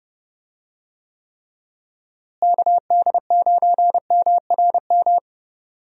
Morse code, heard as a keyed tone, about 30 wpm, 700 Hz. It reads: KD9MRM
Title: USA Call Signs - 6 Characters - Repeated 2x - 30wpm